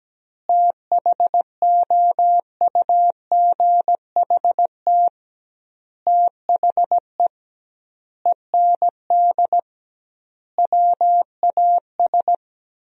Morse code, heard as a keyed tone, about 17 wpm, 700 Hz. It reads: THOUGHT THE END WAS
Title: 2x Word-Spacing - Sentences from Top 300 Words - 17wpm